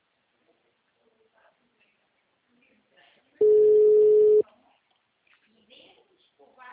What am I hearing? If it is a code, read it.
Ringback tone (Europe, 425 Hz), with speech.